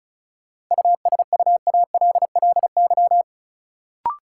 Morse code, heard as a keyed tone, about 36 wpm, 700 Hz. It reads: USUALLY
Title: Sentences from Top 1400 Words - New Word Drill Alt - 35wpm